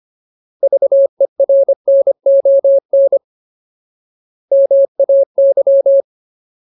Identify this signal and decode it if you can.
Morse code, heard as a keyed tone, about 25 wpm, 550 Hz. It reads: VERNON MAY
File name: Ultimate Word List / 25WPM Ultimate